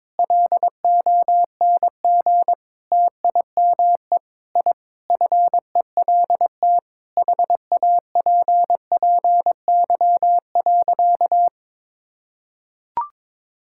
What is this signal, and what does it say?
Morse code, heard as a keyed tone, about 22 wpm, 700 Hz. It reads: LONG TIME I FELT HAPPY.
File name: practice / 22wpm / Sentences with 8-Word Phrases - 22wpm